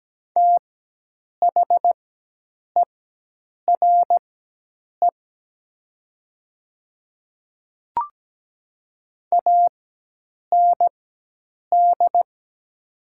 Morse code, heard as a keyed tone, about 17 wpm, spaced as for 8 wpm, 700 Hz. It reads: THERE AND
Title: 4x Character Spacing - Top 100 Words - 17wpm